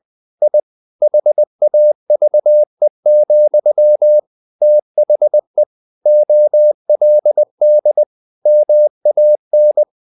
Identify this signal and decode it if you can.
Morse code, heard as a keyed tone, about 20 wpm, 600 Hz. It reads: I HAVE, THE OLD MAN
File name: the old man and the sea / the old man and the sea part 1 20wpm